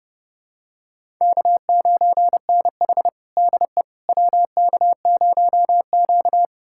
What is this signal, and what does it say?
Morse code, heard as a keyed tone, about 32 wpm, 700 Hz. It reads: K9NH DE WK0Q
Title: USA Callsign DE Callsign - 4 Characters - 30wpm